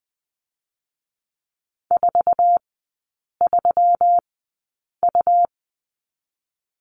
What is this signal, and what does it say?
Morse code, heard as a keyed tone, about 20 wpm, spaced as for 8 wpm, 700 Hz. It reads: 43U
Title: 127 Lesson 21e - Number 6 - Three Characters 20wpm